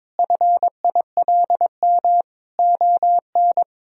Morse code, heard as a keyed tone, about 22 wpm, 700 Hz. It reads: FILM ON
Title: Sentences from Top 1000 Words - Review Part 3 - 22wpm